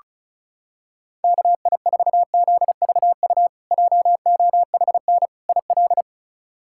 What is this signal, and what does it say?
Morse code, heard as a keyed tone, about 34 wpm, 700 Hz. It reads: KI4ZVU JOHN IL
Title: K1USN SST QSO Practice - 35wpm